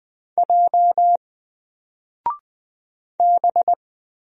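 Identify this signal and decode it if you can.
Morse code, heard as a keyed tone, about 20 wpm, 700 Hz. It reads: J B
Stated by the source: Single letter and number Morse drill at 20 wpm, from N4LCD.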